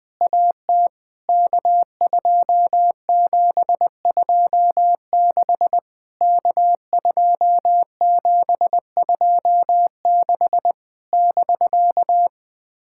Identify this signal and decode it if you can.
Morse code, heard as a keyed tone, about 20 wpm, 700 Hz. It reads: AT K2726 K2726 <BK>
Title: POTA QSO Practice - 20wpm